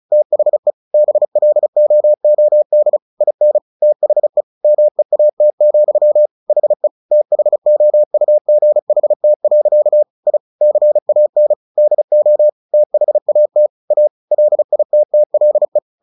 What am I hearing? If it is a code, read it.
Morse code, heard as a keyed tone, about 34 wpm, 600 Hz. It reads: THE BLOOD IN THE MEAT, HE THOUGHT. I CAN DO THAT A LITTLE
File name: the old man and the sea 2 35wpm